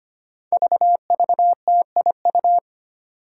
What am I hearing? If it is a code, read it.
Morse code, heard as a keyed tone, about 25 wpm, 700 Hz. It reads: VVTIU
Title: Five Challenging Letter-Number Combinations 25wpm